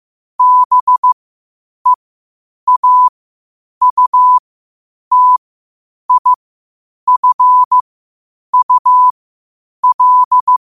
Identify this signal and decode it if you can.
Morse code, heard as a keyed tone, about 15 wpm, spaced as for 8 wpm, 1000 Hz. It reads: BEAUTIFUL
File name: qotdfiles / 8wpm